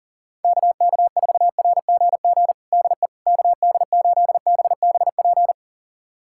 Morse code, heard as a keyed tone, about 40 wpm, 700 Hz. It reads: KK4RGG DE KD8BBP
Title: USA Callsign DE Callsign 40wpm